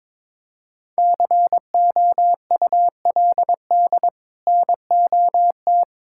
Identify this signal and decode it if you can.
Morse code, heard as a keyed tone, about 22 wpm, 700 Hz. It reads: COULD NOT
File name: practice / 22wpm / Sets of 7 Words - 22wpm